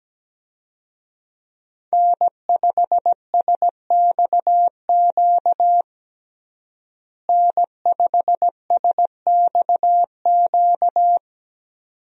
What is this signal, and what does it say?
Morse code, heard as a keyed tone, about 17 wpm, 700 Hz. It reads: N5SXQ N5SXQ
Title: USA Call Signs - 5 Characters - Repeated 2x - 17wpm